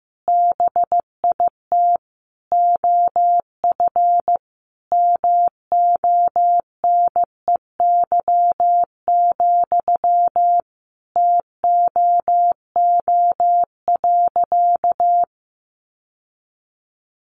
Morse code, heard as a keyed tone, about 15 wpm, 700 Hz. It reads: BIT OF MONEY, TOO.
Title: Sentences from Top 900 Words 15wpm